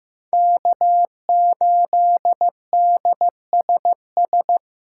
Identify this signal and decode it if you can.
Morse code, heard as a keyed tone, about 15 wpm, 700 Hz. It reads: K8DSS
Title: K1USN SST QSO Practice - 15wpm